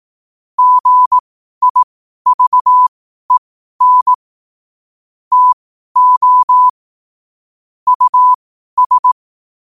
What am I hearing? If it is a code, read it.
Morse code, heard as a keyed tone, about 18 wpm, spaced as for 12 wpm, 1000 Hz. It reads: GIVEN TO US
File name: qotdfiles / 12wpm